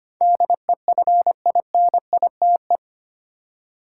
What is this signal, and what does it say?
Morse code, heard as a keyed tone, about 25 wpm, 700 Hz. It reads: DEFINITE
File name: Eight Letter Words - 25wpm